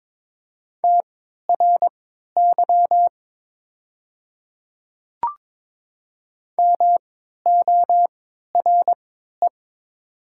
Morse code, heard as a keyed tone, about 22 wpm, spaced as for 13 wpm, 700 Hz. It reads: TRY MORE
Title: 3x Character Spacing - Top 300 Words - 22wpm